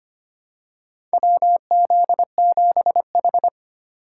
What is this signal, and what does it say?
Morse code, heard as a keyed tone, about 25 wpm, 700 Hz. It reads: WZ7H
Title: USA Call Signs - 4 Characters - 3x Character Spacing - 25wpm